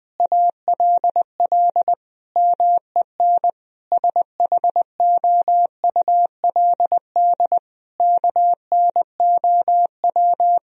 Morse code, heard as a keyed tone, about 20 wpm, 700 Hz. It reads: ALL MEN SHOULD KNOW
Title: Sentences from Top 1000 Words - Review Part 1 - 20wpm